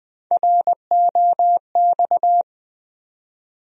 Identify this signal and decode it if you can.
Morse code, heard as a keyed tone, about 20 wpm, 700 Hz. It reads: ROX
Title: USA Call Sign Postfixes - 3 Letters 20wpm